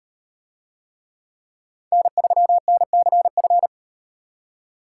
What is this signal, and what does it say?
Morse code, heard as a keyed tone, about 38 wpm, 700 Hz. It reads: N3NCF